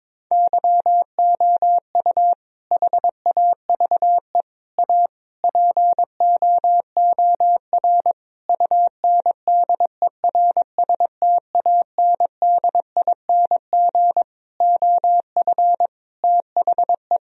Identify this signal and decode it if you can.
Morse code, heard as a keyed tone, about 22 wpm, 700 Hz. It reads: YOU HAVE A POOR UNDERSTANDING OF THE
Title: Sentences from Top 1000 Words - Review Part 3 - 22wpm